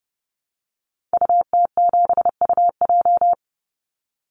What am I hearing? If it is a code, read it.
Morse code, heard as a keyed tone, about 30 wpm, 700 Hz. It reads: UT7UJ
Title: Call Signs Repeated 2x - 30wpm